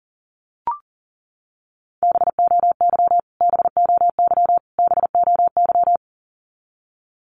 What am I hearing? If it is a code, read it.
Morse code, heard as a keyed tone, about 40 wpm, 700 Hz. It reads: BOY BOY BOY
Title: Top 500 Words Repeated 40wpm